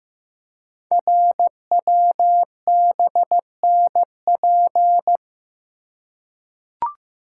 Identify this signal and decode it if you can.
Morse code, heard as a keyed tone, about 15 wpm, 700 Hz. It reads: RWBNP
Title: Code Groups - 5 Letters - 15wpm